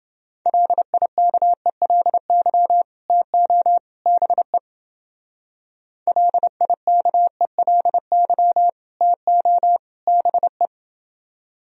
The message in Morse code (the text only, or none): LIKELY TO BE LIKELY TO BE